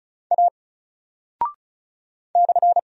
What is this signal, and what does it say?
Morse code, heard as a keyed tone, about 34 wpm, 700 Hz. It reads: A /
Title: Single Character (Letter:Number:Punctuation:Prosign) Alt 35wpm